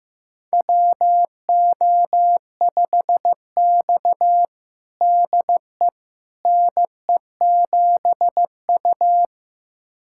Morse code, heard as a keyed tone, about 15 wpm, 700 Hz. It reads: WO5X DE NE7U
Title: USA Callsign DE Callsign - 4 Characters - 15wpm